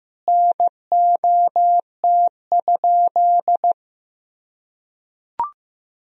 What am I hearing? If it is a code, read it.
Morse code, heard as a keyed tone, about 16 wpm, 700 Hz. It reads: NOT?
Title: Sentences from Top 1000 Words - Review Part 1 - 15wpm